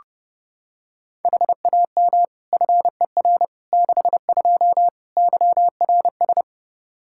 Morse code, heard as a keyed tone, about 32 wpm, 700 Hz. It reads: HAM FER 62 YRS
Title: Practice QSO - Exchange 3 - More Info - 30wpm